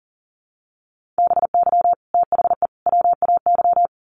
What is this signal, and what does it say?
Morse code, heard as a keyed tone, about 40 wpm, 700 Hz. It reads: BY THE WAY
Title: Sets of 3 Words 40wpm